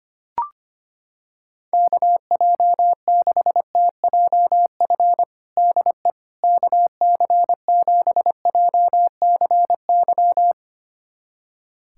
Morse code, heard as a keyed tone, about 25 wpm, 700 Hz. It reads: KJ6TJF DE KC7JCY
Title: USA Callsign DE Callsign - 6 Characters - 25wpm